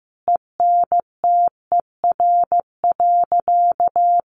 Morse code, heard as a keyed tone, about 15 wpm, 700 Hz. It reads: ENTER.
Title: Sentences from Top 1300 Words 15wpm